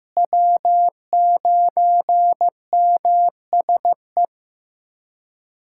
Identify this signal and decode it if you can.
Morse code, heard as a keyed tone, about 15 wpm, 700 Hz. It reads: W9MSE
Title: USA Callsign DE Callsign - 5 Characters - 15wpm